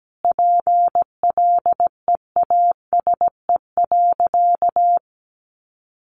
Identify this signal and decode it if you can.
Morse code, heard as a keyed tone, about 17 wpm, 700 Hz. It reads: PLEASE.
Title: Sentences from Top 1500 Words 17wpm